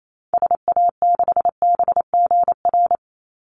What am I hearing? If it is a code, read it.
Morse code, heard as a keyed tone, about 28 wpm, 700 Hz. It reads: SA6BGR
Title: Call Signs - 28wpm